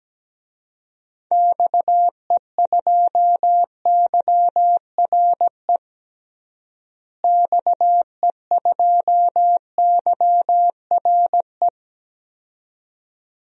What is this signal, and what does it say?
Morse code, heard as a keyed tone, about 17 wpm, 700 Hz. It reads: XE2YRE XE2YRE
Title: DXCC-100 Call Signs - 6 Characters - Repeated 2x - 17wpm